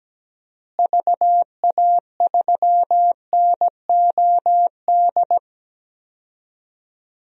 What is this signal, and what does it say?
Morse code, heard as a keyed tone, about 17 wpm, 700 Hz. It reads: VA3NOD
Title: DXCC-100 Call Signs - 6 Characters - Repeated 2x - 17wpm